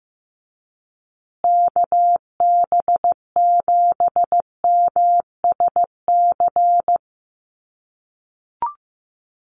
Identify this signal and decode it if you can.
Morse code, heard as a keyed tone, about 15 wpm, 700 Hz. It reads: KB7MSC
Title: USA Call Signs Repeated 2x 15wpm